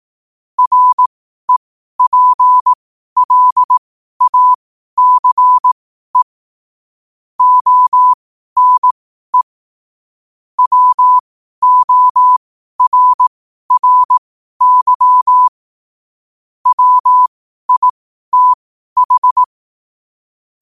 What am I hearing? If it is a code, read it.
Morse code, heard as a keyed tone, about 18 wpm, spaced as for 12 wpm, 1000 Hz. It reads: REPLACE ONE WORRY WITH
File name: qotdfiles / 12wpm